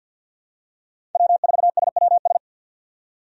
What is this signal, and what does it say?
Morse code, heard as a keyed tone, about 47 wpm, 700 Hz. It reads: W4SWS
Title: USA Call Signs - 5 Characters - 50wpm